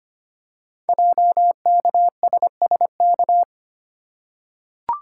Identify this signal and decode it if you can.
Morse code, heard as a keyed tone, about 25 wpm, 700 Hz. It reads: JKSSK
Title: Code Groups - 5 Letter:Numbers - 25wpm